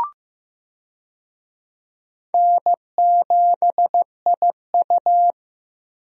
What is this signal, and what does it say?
Morse code, heard as a keyed tone, about 16 wpm, 700 Hz. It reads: N7IU
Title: USA Call Signs - 4 Characters - Repeated 3x - 15wpm